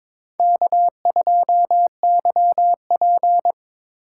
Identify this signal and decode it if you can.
Morse code, heard as a keyed tone, about 21 wpm, 700 Hz. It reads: K2YP
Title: USA Call Signs - 4 Characters - 4x Character Spacing - 22wpm